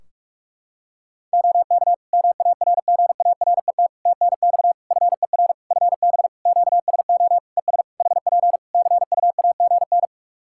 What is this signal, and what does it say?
Morse code, heard as a keyed tone, about 46 wpm, 700 Hz. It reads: OK MARGARET TNX FER FB QSO ES HP CUAGN
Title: Practice QSO - Exchange 4 - Ending - 45wpm